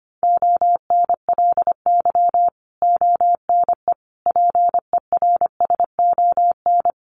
Morse code, heard as a keyed tone, about 25 wpm, 700 Hz. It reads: ONLY ONE PERSON
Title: Sentences from Top 1100 Words 25wpm